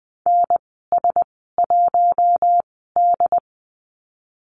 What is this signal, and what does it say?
Morse code, heard as a keyed tone, about 20 wpm, spaced as for 14 wpm, 700 Hz. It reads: NS1D
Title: 122 Lesson 20g - Number 3 - Call Signs 20wpm